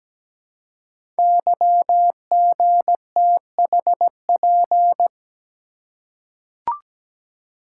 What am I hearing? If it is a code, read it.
Morse code, heard as a keyed tone, about 17 wpm, 700 Hz. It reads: YGTHP
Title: Code Groups - 5 Letters - 17wpm